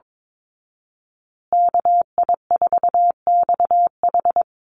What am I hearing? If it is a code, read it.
Morse code, heard as a keyed tone, about 22 wpm, 700 Hz. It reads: KI4XH
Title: Call Signs Repeated 3x - 22wpm